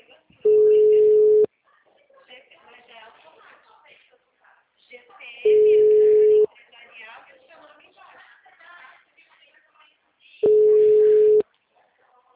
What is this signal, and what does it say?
Ringback tone (Europe, 425 Hz), with speech.